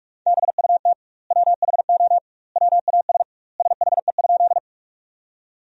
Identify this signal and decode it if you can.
Morse code, heard as a keyed tone, about 46 wpm, 700 Hz. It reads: BUT WHO WAS SHE?
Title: 2x Word-Spacing - Sentences from Top 100 Words - 45wpm